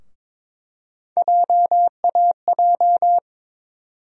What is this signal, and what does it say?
Morse code, heard as a keyed tone, about 22 wpm, 700 Hz. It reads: JAJ
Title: USA Call Sign Postfixes - 3 Letters 22wpm